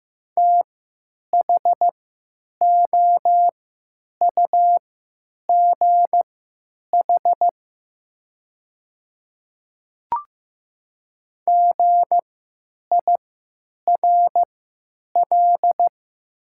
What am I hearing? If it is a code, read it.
Morse code, heard as a keyed tone, about 15 wpm, spaced as for 8 wpm, 700 Hz. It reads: THOUGH GIRL
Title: 3x Character Spacing - Top 500 Words - 15wpm